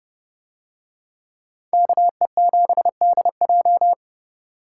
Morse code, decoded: KE7DJ

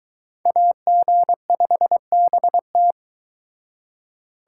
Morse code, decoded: AG5BT

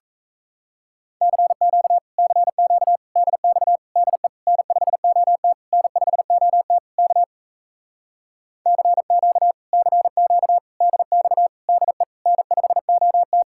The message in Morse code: CQ CQ DX DE N5OT N5OT K CQ CQ DX DE N5OT

